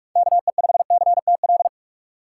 Morse code, heard as a keyed tone, about 46 wpm, 700 Hz. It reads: KE5CTL
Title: USA Call Signs - 6 Characters - 6x Character Spacing - 45wpm